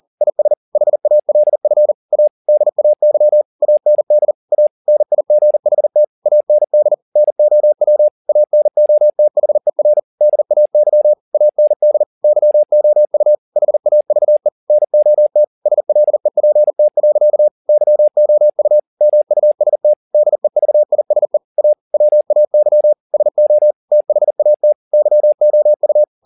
Morse code, heard as a keyed tone, about 40 wpm, 600 Hz. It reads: IS HALF A DAY AND A NIGHT AND NOW ANOTHER DAY AND YOU HAVE NOT SLEPT. YOU MUST DEVISE A WAY SO THAT YOU
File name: the old man and the sea 2 40wpm